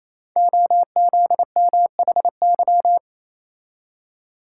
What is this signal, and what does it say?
Morse code, heard as a keyed tone, about 27 wpm, 700 Hz. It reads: OZMHY
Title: Code Groups - 5 Letters - 28wpm